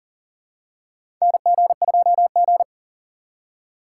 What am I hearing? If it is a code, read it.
Morse code, heard as a keyed tone, about 40 wpm, 700 Hz. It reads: NG2G
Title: USA Call Signs - 4 Characters - 3x Character Spacing - 40wpm